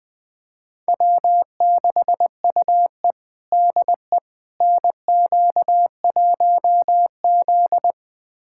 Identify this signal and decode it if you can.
Morse code, heard as a keyed tone, about 20 wpm, 700 Hz. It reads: W6UE DE NQ1Z
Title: USA Callsign DE Callsign - 4 Characters - 20wpm